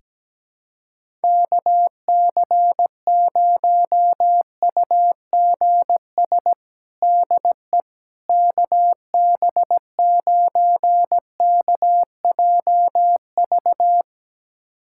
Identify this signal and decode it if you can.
Morse code, heard as a keyed tone, about 17 wpm, 700 Hz. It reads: KC0UGS DE KB9KJV
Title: USA Callsign DE Callsign - 6 Characters - 17wpm